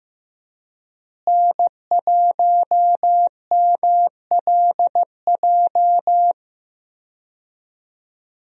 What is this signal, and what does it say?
Morse code, heard as a keyed tone, about 15 wpm, 700 Hz. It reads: N1MLJ
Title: USA Call Signs - 5 Characters - Repeated 3x - 15wpm